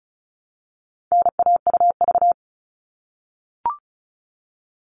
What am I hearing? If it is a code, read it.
Morse code, heard as a keyed tone, about 34 wpm, 700 Hz. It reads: NAUV
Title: All Cut Numbers - 4-Digit 35wpm